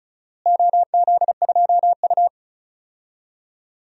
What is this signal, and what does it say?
Morse code, heard as a keyed tone, about 36 wpm, 700 Hz. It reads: OZ2U
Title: DXCC-100 Call Signs - 4 Characters - Repeated 2x - 35wpm